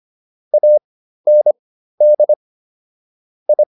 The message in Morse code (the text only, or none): AND I